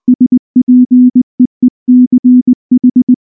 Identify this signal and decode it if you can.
Morse code, heard as a keyed tone, about 19 wpm, 250 Hz. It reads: SPEECH
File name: morse / morse-training-20WPM-10